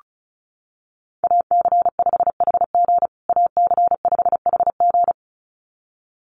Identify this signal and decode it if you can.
Morse code, heard as a keyed tone, about 34 wpm, 700 Hz. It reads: AC5HG AC5HG
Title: USA Call Signs Repeated 2x 35wpm